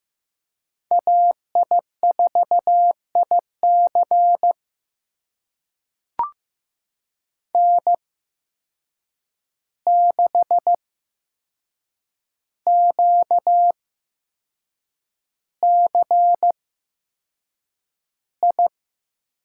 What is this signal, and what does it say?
Morse code, heard as a keyed tone, about 15 wpm, spaced as for 6 wpm, 700 Hz. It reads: AI4IC N6QCI
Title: USA Call Signs - 5 Characters - 8x Character Spacing - 15wpm